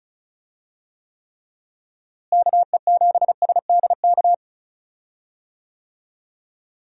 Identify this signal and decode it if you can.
Morse code, heard as a keyed tone, about 34 wpm, 700 Hz. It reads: KE7SDK
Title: USA Call Signs - 6 Characters - Repeated 2x - 35wpm